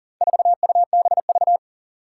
Morse code, heard as a keyed tone, about 40 wpm, 700 Hz. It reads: 4UBV
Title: Four Challenging Letter-Number Combinations 40wpm